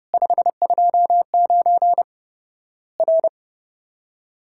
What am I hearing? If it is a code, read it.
Morse code, heard as a keyed tone, about 30 wpm, 700 Hz. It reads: 529 R
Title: Worked All Europe DX Contest - QTC Exchange - 30wpm